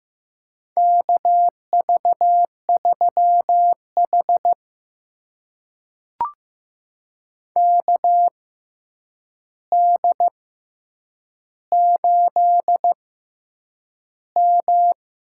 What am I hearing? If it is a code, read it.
Morse code, heard as a keyed tone, about 15 wpm, spaced as for 8 wpm, 700 Hz. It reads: KV3H KD8M
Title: USA Call Signs - 4 Characters - 6x Character Spacing - 15wpm